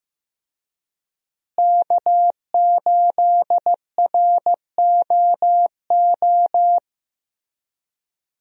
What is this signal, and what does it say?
Morse code, heard as a keyed tone, about 15 wpm, 700 Hz. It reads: K8ROO
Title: USA Call Signs - 5 Characters - Repeated 3x - 15wpm